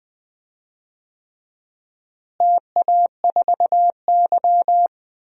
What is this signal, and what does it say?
Morse code, heard as a keyed tone, about 20 wpm, 700 Hz. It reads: TA4Y
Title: DXCC-100 Call Signs - 4 Characters - Repeated 3x - 20wpm